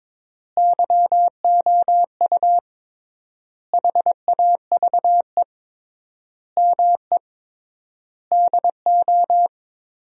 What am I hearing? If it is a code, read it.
Morse code, heard as a keyed tone, about 22 wpm, 700 Hz. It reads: YOU HAVE ME DO
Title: 3x Word-Spacing - Sentences from Top 300 Words - 22wpm